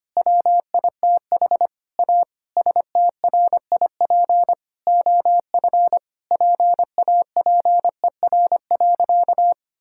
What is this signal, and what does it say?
Morse code, heard as a keyed tone, about 25 wpm, 700 Hz. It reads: WITH A STRIP OF PAPER.
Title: Sentences from Top 1600 Words 25wpm